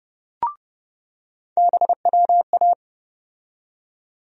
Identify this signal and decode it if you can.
Morse code, heard as a keyed tone, about 30 wpm, 700 Hz. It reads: BWA
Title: USA Call Sign Postfixes - 3 Letters 30wpm